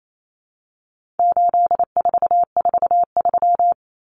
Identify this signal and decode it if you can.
Morse code, heard as a keyed tone, about 28 wpm, 700 Hz. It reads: 8443
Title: Common Cut Numbers - 4-Digit 28wpm